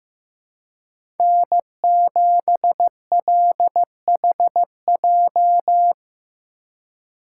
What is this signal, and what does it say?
Morse code, heard as a keyed tone, about 16 wpm, 700 Hz. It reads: N7LHJ
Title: USA Call Signs - 5 Characters - 2x Character Spacing - 15wpm